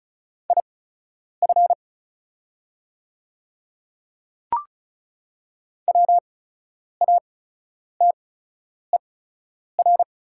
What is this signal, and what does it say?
Morse code, heard as a keyed tone, about 36 wpm, spaced as for 10 wpm, 700 Hz. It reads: IF WATER
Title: 8x Character Spacing - Top 100 Words - 35wpm